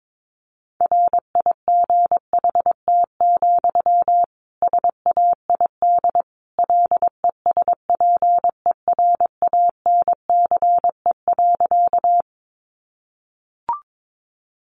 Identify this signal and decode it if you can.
Morse code, heard as a keyed tone, about 22 wpm, 700 Hz. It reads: RIGHT, SAID LESPERANCE.
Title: A Sound Of Thunder 22wpm